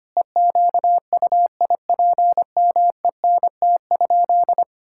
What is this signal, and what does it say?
Morse code, heard as a keyed tone, about 25 wpm, 700 Hz. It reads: EQUIPMENT?
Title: Sentences from Top 1000 Words - Review Part 6 - 25wpm